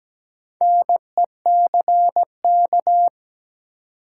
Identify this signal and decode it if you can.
Morse code, heard as a keyed tone, about 17 wpm, 700 Hz. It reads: NECK
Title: Sets of 5 Words Encore - New Word Drill - 17wpm